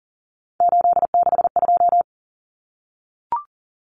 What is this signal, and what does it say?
Morse code, heard as a keyed tone, about 40 wpm, 700 Hz. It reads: 862
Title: Numbers - 3-Digit 40wpm